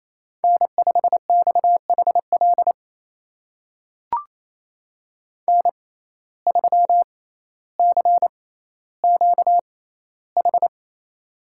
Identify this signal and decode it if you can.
Morse code, heard as a keyed tone, about 28 wpm, spaced as for 14 wpm, 700 Hz. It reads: N5XHL N3CQH
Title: USA Call Signs - 5 Characters - 6x Character Spacing - 28wpm